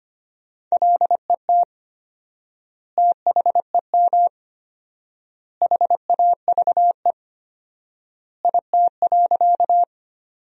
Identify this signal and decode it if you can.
Morse code, heard as a keyed tone, about 25 wpm, 700 Hz. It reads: LET THEM HAVE IT.
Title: 4x Word-Spacing - Sentences from Top 300 Words - 25wpm